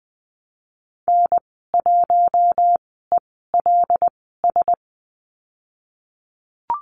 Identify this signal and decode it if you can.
Morse code, heard as a keyed tone, about 20 wpm, spaced as for 15 wpm, 700 Hz. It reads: N1ELS
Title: 082 Lesson 14g - Number 2 - Call Signs 20wpm